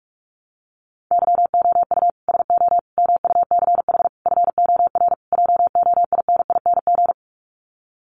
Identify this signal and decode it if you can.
Morse code, heard as a keyed tone, about 44 wpm, 700 Hz. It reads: YOU SO MUCH FOR JOINING
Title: Sets of 5 Words Encore2 45wpm